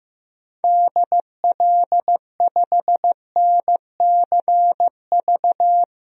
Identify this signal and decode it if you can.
Morse code, heard as a keyed tone, about 16 wpm, 700 Hz. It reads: DL5NCV
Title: DXCC-100 Call Signs - 6 Characters - 15wpm